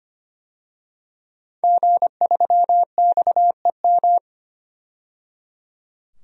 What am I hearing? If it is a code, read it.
Morse code, heard as a keyed tone, about 25 wpm, 700 Hz. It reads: G3XEM
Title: DXCC-100 Call Signs - 5 Characters - 25wpm